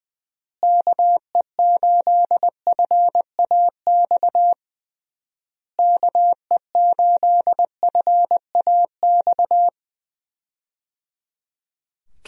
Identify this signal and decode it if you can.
Morse code, heard as a keyed tone, about 21 wpm, 700 Hz. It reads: KE8FAX KE8FAX
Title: USA Call Signs - 6 Characters - Repeated 3x - 20wpm